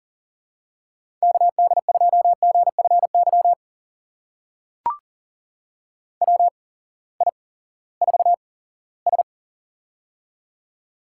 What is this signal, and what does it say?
Morse code, heard as a keyed tone, about 42 wpm, spaced as for 20 wpm, 700 Hz. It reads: KD2GFY WI4S
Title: USA Call Signs - 8x Character Spacing - 40wpm